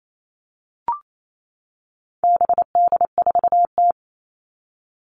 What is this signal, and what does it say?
Morse code, heard as a keyed tone, about 28 wpm, 700 Hz. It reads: BD4T
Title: All Cut Numbers - 4-Digit 28wpm